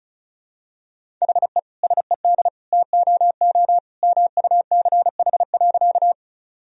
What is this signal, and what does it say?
Morse code, heard as a keyed tone, about 34 wpm, 700 Hz. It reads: HE SED TOO MUCH.
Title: Sentences from Top 200 Words and Abbreviations 35wpm